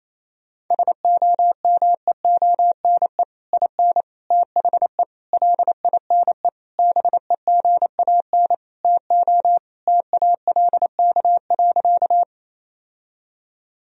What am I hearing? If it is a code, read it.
Morse code, heard as a keyed tone, about 28 wpm, 700 Hz. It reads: SOMEONE IN THE LINE BEGAN TO TALK.
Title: Sentences from Top 1000 Words - Review Part 1 - 28wpm